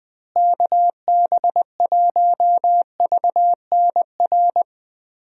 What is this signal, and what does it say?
Morse code, heard as a keyed tone, about 20 wpm, 700 Hz. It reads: KB1VNR
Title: USA Call Signs - 6 Characters - Repeated 3x - 20wpm